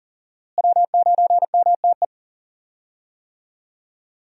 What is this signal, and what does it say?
Morse code, heard as a keyed tone, about 40 wpm, 700 Hz. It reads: W9MTE
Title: USA Call Signs - 5 Characters - Repeated 2x - 40wpm